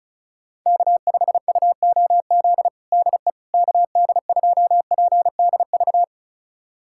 Morse code, heard as a keyed tone, about 36 wpm, 700 Hz. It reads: K5UOZ DE KD2PDV
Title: USA Callsign DE Callsign 35wpm